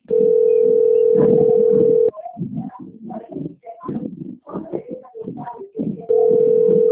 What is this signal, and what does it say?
Ringback tone (North America), 440 + 480 Hz.